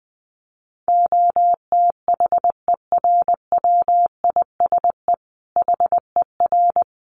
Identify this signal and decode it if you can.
Morse code, heard as a keyed tone, about 20 wpm, 700 Hz. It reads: OTHERWISE HER
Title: Sentences from Top 1200 Words 20wpm